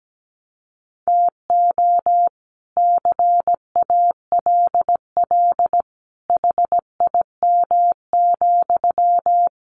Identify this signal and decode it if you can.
Morse code, heard as a keyed tone, about 17 wpm, 700 Hz. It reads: TO CALL HIM,
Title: Sentences from Top 1400 Words 17wpm